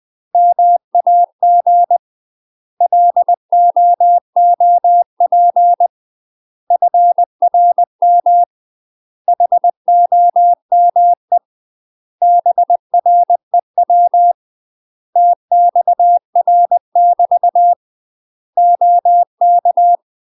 Morse code, heard as a keyed tone, about 20 wpm, 700 Hz. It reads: MAG LOOP FRM HOME BREW TXR <BT> OK
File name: UK MASTER QSOs TRAINING / 20WPM / UK MASTER QSOs 0020